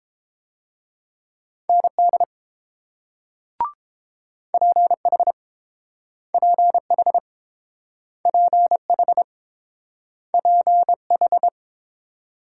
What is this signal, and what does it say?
Morse code, heard as a keyed tone, about 32 wpm, 700 Hz. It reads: ND PH PH PH PH